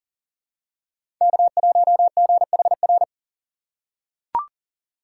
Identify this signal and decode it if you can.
Morse code, heard as a keyed tone, about 37 wpm, 700 Hz. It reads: K1GHR
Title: USA Call Signs - 5 Characters - 2x Character Spacing - 40wpm